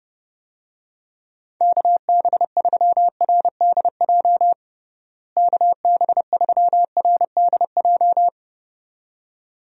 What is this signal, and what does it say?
Morse code, heard as a keyed tone, about 30 wpm, 700 Hz. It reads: KB3RDJ KB3RDJ
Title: USA Call Signs - 6 Characters - Repeated 2x - 30wpm